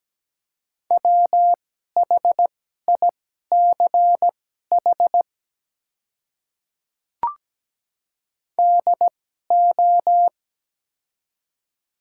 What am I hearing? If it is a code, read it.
Morse code, heard as a keyed tone, about 17 wpm, spaced as for 12 wpm, 690 Hz. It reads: WHICH DO
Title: 2x Character Spacing - Top 100 Words - 17wpm